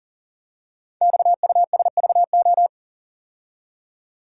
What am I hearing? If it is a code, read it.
Morse code, heard as a keyed tone, about 39 wpm, 700 Hz. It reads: XUSVO
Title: Code Groups - 5 Letter:Numbers - 40wpm